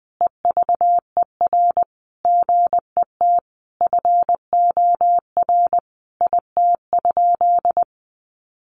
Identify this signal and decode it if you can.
Morse code, heard as a keyed tone, about 20 wpm, 700 Hz. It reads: EVER GET FOR IT?
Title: Sentences from Top 400 Words 20wpm